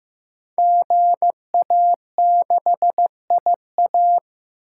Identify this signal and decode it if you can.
Morse code, heard as a keyed tone, about 15 wpm, 700 Hz. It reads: GA6IA
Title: Code Groups - 5 Letter:Numbers - 15wpm